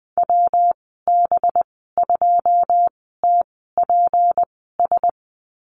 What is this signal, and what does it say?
Morse code, heard as a keyed tone, about 20 wpm, spaced as for 15 wpm, 700 Hz. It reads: WB2TPS
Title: 238 Lesson 38g - Letter q - Call Signs 20wpm